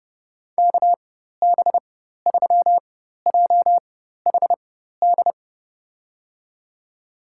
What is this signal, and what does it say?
Morse code, heard as a keyed tone, about 32 wpm, spaced as for 14 wpm, 700 Hz. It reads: KB3JHD
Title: USA Call Signs - 6 Characters - 4x Character Spacing - 30wpm